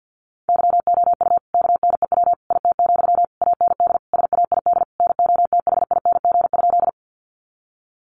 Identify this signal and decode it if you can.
Morse code, heard as a keyed tone, about 50 wpm, 700 Hz. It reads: YOU KNEW IT, AND SAID NOTHING?
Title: Sentences from Top 500 Words 50wpm